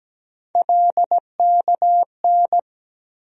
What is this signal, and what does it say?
Morse code, heard as a keyed tone, about 17 wpm, 700 Hz. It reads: LKN
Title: USA Call Sign Postfixes - 3 Letters 17wpm